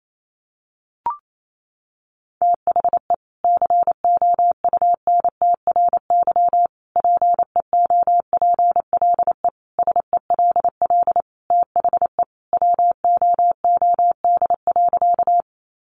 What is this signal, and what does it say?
Morse code, heard as a keyed tone, about 28 wpm, 700 Hz. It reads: THE COUNTRY PEOPLE SELL THE WOOD.
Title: Sentences from Top 1100 Words 28wpm